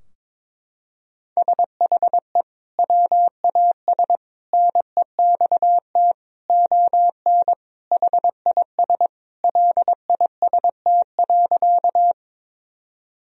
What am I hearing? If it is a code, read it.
Morse code, heard as a keyed tone, about 22 wpm, 700 Hz. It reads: SHE WAS NEXT ON HIS LIST.
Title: Sentences from Top 1000 Words - Review Part 2 - 22wpm